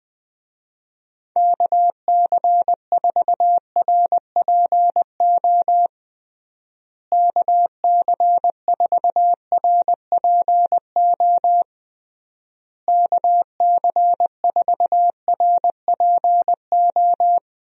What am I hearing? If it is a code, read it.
Morse code, heard as a keyed tone, about 20 wpm, 700 Hz. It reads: KC4RPO KC4RPO KC4RPO
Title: USA Call Signs - 6 Characters - Repeated 3x - 20wpm